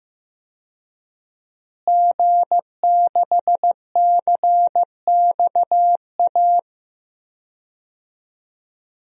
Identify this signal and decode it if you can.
Morse code, heard as a keyed tone, about 15 wpm, 700 Hz. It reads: G6CXA
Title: DXCC-100 Call Signs - 5 Characters - Repeated 2x - 15wpm